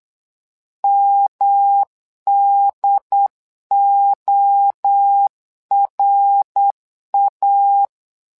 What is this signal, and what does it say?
Morse code, heard as a keyed tone, about 8 wpm, 800 Hz. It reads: MDORA
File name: LEZIONI / 3 MWRO / 3 MWRO 8wpm 2606